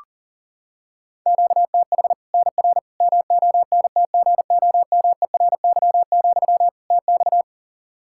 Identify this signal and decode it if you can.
Morse code, heard as a keyed tone, about 37 wpm, 700 Hz. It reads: QTH NR MONTGOMERY, TX
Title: Practice QSO - Exchange 1 - Beginning - 40wpm